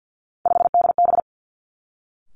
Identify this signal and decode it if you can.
Morse code, heard as a keyed tone, about 50 wpm, 700 Hz. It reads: 5DB
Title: Three Challenging Letter-Number Combinations 50wpm